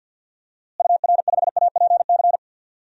Morse code, heard as a keyed tone, about 49 wpm, 700 Hz. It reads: UR5APC
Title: DXCC-100 Call Signs - 6 Characters - Repeated 2x - 50wpm